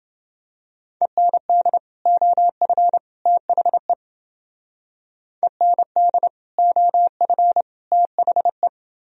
Morse code, heard as a keyed tone, about 28 wpm, 700 Hz. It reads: END OF THE END OF THE